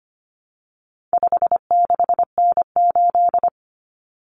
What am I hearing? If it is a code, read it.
Morse code, heard as a keyed tone, about 25 wpm, 700 Hz. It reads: 56N8
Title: Common Cut Numbers - 4-Digit 25wpm